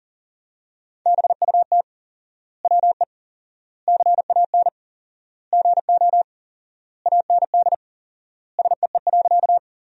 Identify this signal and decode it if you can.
Morse code, heard as a keyed tone, about 40 wpm, 700 Hz. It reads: BUT WE CAN GO AND SEE.
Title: 4x Word-Spacing - Sentences from Top 100 Words - 40wpm